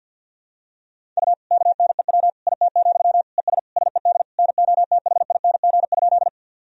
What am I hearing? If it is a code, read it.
Morse code, heard as a keyed tone, about 50 wpm, 700 Hz. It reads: U KNEW IT, ES SED NOTHING?
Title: Sentences from Top 500 Words and Abbreviations 50wpm